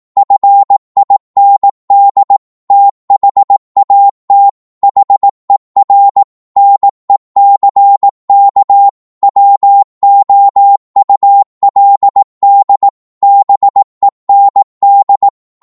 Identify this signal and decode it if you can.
Morse code, heard as a keyed tone, about 18 wpm, 800 Hz. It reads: FIND THAT HER NECK WOULD BEND
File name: 18wpm